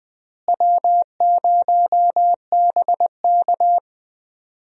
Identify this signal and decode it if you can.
Morse code, heard as a keyed tone, about 20 wpm, 700 Hz. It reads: W0BK
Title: USA Call Signs - 4 Characters - Repeated 2x - 20wpm